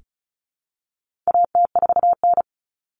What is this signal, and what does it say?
Morse code, heard as a keyed tone, about 34 wpm, 700 Hz. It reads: AT4N
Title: Common Cut Numbers - 4-Digit 35wpm